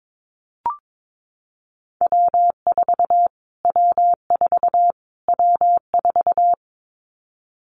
Morse code, heard as a keyed tone, about 22 wpm, 700 Hz. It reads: W4 W4 W4
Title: Callsign Prefixes for Top 25 DXCC - Repeated 3x - 22wpm